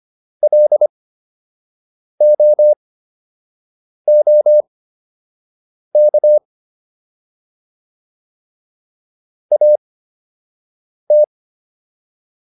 Morse code, heard as a keyed tone, about 25 wpm, spaced as for 6 wpm, 600 Hz. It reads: LOOK AT